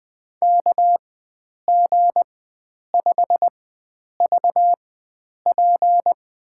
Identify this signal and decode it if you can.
Morse code, heard as a keyed tone, about 20 wpm, spaced as for 9 wpm, 700 Hz. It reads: KG5VP
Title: USA Call Signs - 4x Character Spacing - 20wpm